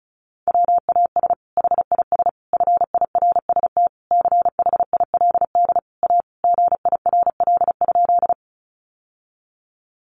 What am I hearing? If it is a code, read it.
Morse code, heard as a keyed tone, about 34 wpm, 700 Hz. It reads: WAS HIS FIRST CHILD A GIRL?
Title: Sentences from Top 800 Words 35wpm